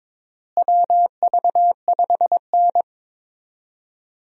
Morse code, heard as a keyed tone, about 22 wpm, 700 Hz. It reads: WV5N
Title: USA Call Signs - 4 Characters - Repeated 3x - 22wpm